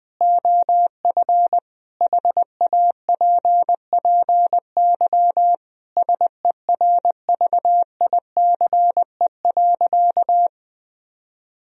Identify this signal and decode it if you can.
Morse code, heard as a keyed tone, about 20 wpm, 700 Hz. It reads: OF HAPPY SERVICE.
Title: Sentences from Top 1000 Words - Review Part 4 - 20wpm